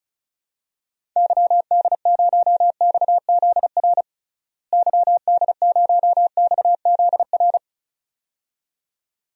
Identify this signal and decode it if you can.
Morse code, heard as a keyed tone, about 34 wpm, 700 Hz. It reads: YD0XZR YD0XZR
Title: DXCC-100 Call Signs - 6 Characters - Repeated 2x - 35wpm